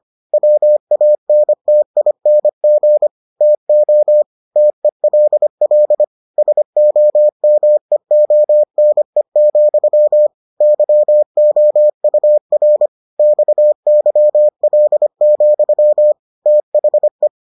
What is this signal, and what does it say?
Morse code, heard as a keyed tone, about 25 wpm, 600 Hz. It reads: WANTING TO TELL SOMEONE, YOUR XYL, THE